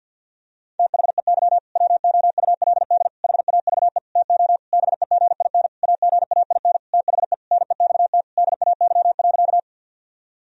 Morse code, heard as a keyed tone, about 51 wpm, 700 Hz. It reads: THEY WOULD HAVE TO BEGIN AGAIN THE NEXT DAY.
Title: Sentences from Top 1000 Words - Review Part 1 - 50wpm